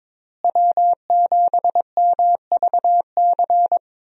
Morse code, heard as a keyed tone, about 22 wpm, 700 Hz. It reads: W7MVC
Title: USA Call Signs - 5 Characters - Repeated 2x - 22wpm